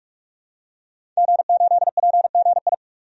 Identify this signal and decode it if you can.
Morse code, heard as a keyed tone, about 47 wpm, 700 Hz. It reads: G8PGI
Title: DXCC-100 Call Signs - 5 Characters - 45wpm